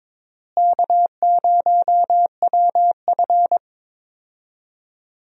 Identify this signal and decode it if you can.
Morse code, heard as a keyed tone, about 22 wpm, 700 Hz. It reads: K0WF
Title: USA Call Signs - 4 Characters - Repeated 3x - 22wpm